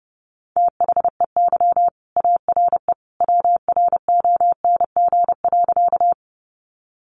Morse code, heard as a keyed tone, about 30 wpm, 700 Hz. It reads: THEY ARE WRONG.
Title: Sentences from Top 900 Words 30wpm